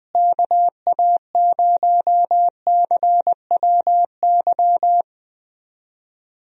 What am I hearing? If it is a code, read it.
Morse code, heard as a keyed tone, about 20 wpm, 700 Hz. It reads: KA0CWY
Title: USA Call Signs - 6 Characters - Repeated 3x - 20wpm